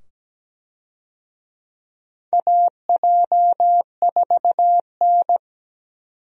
Morse code, heard as a keyed tone, about 17 wpm, 700 Hz. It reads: AJ4N
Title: USA Call Signs - 4 Characters - 17wpm